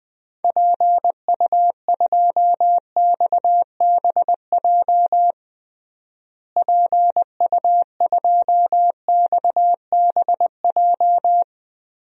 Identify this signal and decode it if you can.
Morse code, heard as a keyed tone, about 21 wpm, 700 Hz. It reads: PU2XBJ PU2XBJ
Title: DXCC-100 Call Signs - 6 Characters - Repeated 2x - 20wpm